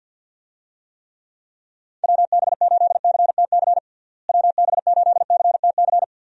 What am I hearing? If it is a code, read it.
Morse code, heard as a keyed tone, about 51 wpm, 700 Hz. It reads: WB8CTC WB8CTC
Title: USA Call Signs - 6 Characters - Repeated 3x - 50wpm